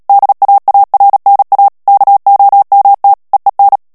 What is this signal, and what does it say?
Morse code, heard as a keyed tone, about 37 wpm, 800 Hz. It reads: DAARNA KOMT EEN
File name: morse mp3 / HOLLANDAIS texte clair 2 vitesse 35 mots minute